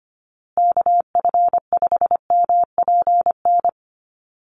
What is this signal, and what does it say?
Morse code, heard as a keyed tone, about 25 wpm, 700 Hz. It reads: KF5MPN
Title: USA Call Signs Repeated 2x 25wpm